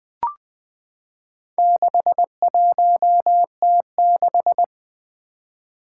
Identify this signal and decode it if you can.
Morse code, heard as a keyed tone, about 20 wpm, 700 Hz. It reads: 61T6
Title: Four Letter-Number Combinations - 20wpm